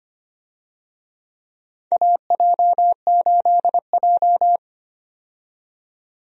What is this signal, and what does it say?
Morse code, heard as a keyed tone, about 25 wpm, 700 Hz. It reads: AJ8J
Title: USA Call Signs - 4 Characters - 25wpm